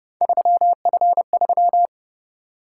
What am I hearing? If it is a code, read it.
Morse code, heard as a keyed tone, about 29 wpm, 700 Hz. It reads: 3F3